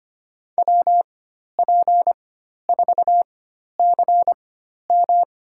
Morse code, heard as a keyed tone, about 25 wpm, spaced as for 12 wpm, 700 Hz. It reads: WP4CM
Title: USA Call Signs - 4x Character Spacing - 25wpm